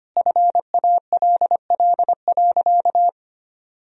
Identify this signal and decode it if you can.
Morse code, heard as a keyed tone, about 25 wpm, 700 Hz. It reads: FALL.
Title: Sentences from Top 1000 Words - Review Part 2 - 25wpm